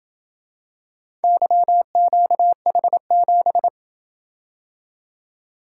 Morse code, heard as a keyed tone, about 27 wpm, 700 Hz. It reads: YQH7